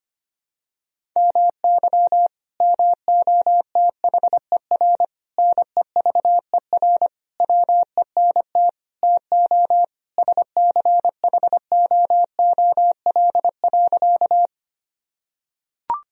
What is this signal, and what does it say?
Morse code, heard as a keyed tone, about 25 wpm, 700 Hz. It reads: MY MOTHER NEVER WENT TO SCHOOL.
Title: Sentences from Top 1000 Words - Review Part 1 - 25wpm